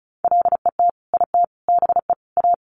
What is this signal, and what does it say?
Morse code, heard as a keyed tone, about 34 wpm, 700 Hz. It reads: LET IT BE A
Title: Sentences from Top 1100 Words 35wpm